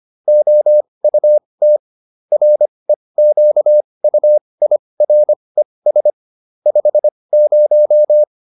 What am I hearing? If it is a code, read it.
Morse code, heard as a keyed tone, about 25 wpm, spaced as for 20 wpm, 600 Hz. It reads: OUT REQUIRES 50